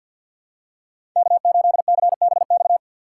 Morse code, heard as a keyed tone, about 52 wpm, 700 Hz. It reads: K8CBX
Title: USA Call Signs - 5 Characters - 3x Character Spacing - 50wpm